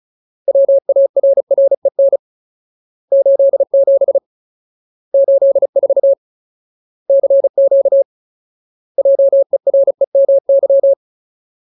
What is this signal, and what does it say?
Morse code, heard as a keyed tone, about 34 wpm, 550 Hz. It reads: WARREN 87 84 CQ JEREMY
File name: Ultimate Word List / 35WPM Ultimate